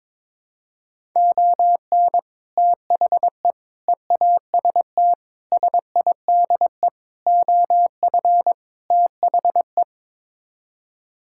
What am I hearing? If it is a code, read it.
Morse code, heard as a keyed tone, about 21 wpm, 700 Hz. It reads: ON THE EAST SIDE OF THE
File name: practice / 22wpm / Sets of 6 Words - 22wpm